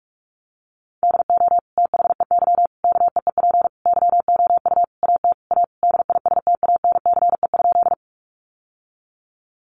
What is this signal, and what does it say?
Morse code, heard as a keyed tone, about 46 wpm, 700 Hz. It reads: DO THEY KEEP YOU AT A DISTANCE?
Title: Sentences from Top 1200 Words 45wpm